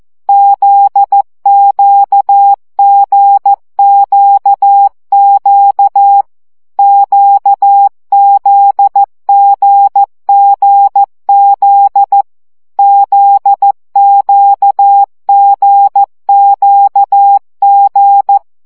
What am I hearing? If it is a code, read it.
Morse code, heard as a keyed tone, about 14 wpm, 800 Hz. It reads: ZQGQQ QZGGZ ZQGQG
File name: CW gyakorlas g z q 12 baud